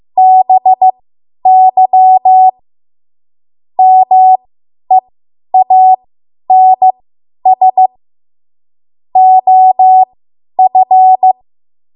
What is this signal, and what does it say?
Morse code, heard as a keyed tone, about 15 wpm, spaced as for 10 wpm, 750 Hz. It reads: BY MEANS OF
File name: Morse / 10WPM